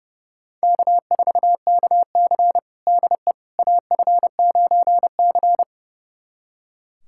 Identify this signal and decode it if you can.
Morse code, heard as a keyed tone, about 32 wpm, 700 Hz. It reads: K4KC DE AF9C
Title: USA Callsign DE Callsign - 4 Characters - 30wpm